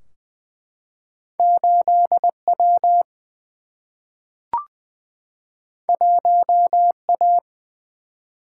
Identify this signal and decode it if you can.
Morse code, heard as a keyed tone, about 20 wpm, 700 Hz. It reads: 8 W 1 A
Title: Single-Letter Single-Number 20wpm